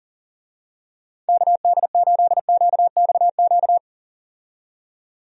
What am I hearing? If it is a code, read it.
Morse code, heard as a keyed tone, about 40 wpm, 700 Hz. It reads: KD8QXQ
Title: USA Call Signs - 6 Characters - 40wpm